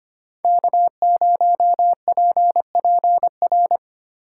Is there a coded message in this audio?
Yes: Morse code from 0.4 to 3.8 s, heard as a keyed tone, about 32 wpm, 700 Hz. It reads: K0PPR